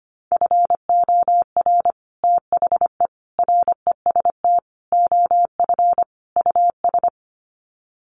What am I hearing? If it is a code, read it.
Morse code, heard as a keyed tone, about 25 wpm, 700 Hz. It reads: FOR THE REST OF US
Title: Sets of 5 Words Encore 25wpm